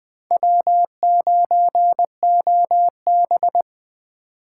Morse code, heard as a keyed tone, about 24 wpm, 700 Hz. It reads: W9OB
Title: Speed Racing - USA 1x2 Call Signs 20wpm